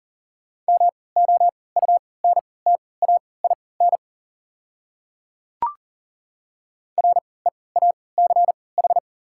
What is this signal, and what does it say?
Morse code, heard as a keyed tone, about 40 wpm, spaced as for 23 wpm, 690 Hz. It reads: MOUNTAIN REACH
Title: 3x Character Spacing - Top 500 Words - 40wpm